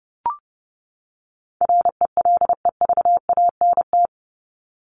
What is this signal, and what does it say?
Morse code, heard as a keyed tone, about 30 wpm, 700 Hz. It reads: RELEVANT
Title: Top 1001-1250 Words 30wpm